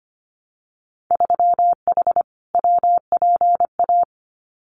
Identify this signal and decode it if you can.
Morse code, heard as a keyed tone, about 25 wpm, 700 Hz. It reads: 3H WPA
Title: Winter Field Day Exchanges 25wpm